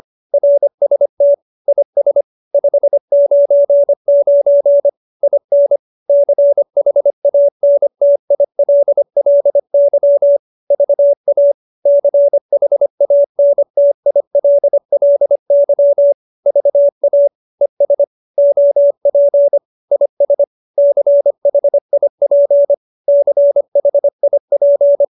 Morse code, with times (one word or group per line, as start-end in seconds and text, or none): 0.3-1.3 RST
1.7-2.2 IS
2.5-4.9 599
5.2-5.8 IN
6.1-10.4 CHANTILLY
10.7-11.5 VA
11.9-16.1 CHANTILLY
16.5-17.3 VA
17.6-18.0 ES
18.4-19.6 OP
19.9-20.4 IS
20.8-22.7 CHIP
23.1-25.0 CHIP